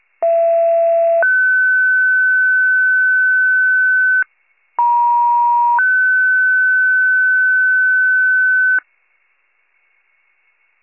Two-tone page, heard as a one-second tone, then a three-second tone, about 670 then 1500 Hz, about 950 then 1500 Hz. It is heard from 0.2 to 8.8 s.